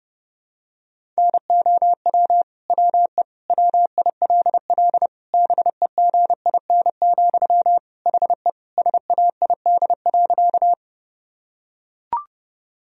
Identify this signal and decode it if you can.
Morse code, heard as a keyed tone, about 30 wpm, 700 Hz. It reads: NOW WE WILL BEGIN, HE SAID.
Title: Sentences from Top 1000 Words - Review Part 1 - 30wpm